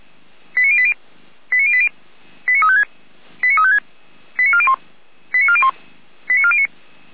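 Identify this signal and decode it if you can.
Five-tone selective call (ZVEI), heard as quick runs of five tones, tones of about 70 ms. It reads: 89080 89080 89356 89356 89401 89401 89409
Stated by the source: Five-tone ZVEI selective call 89338, received by SDR.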